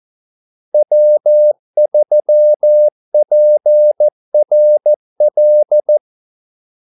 Morse code, heard as a keyed tone, about 14 wpm, 600 Hz. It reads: W3PRL
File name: Callsigns-14wpm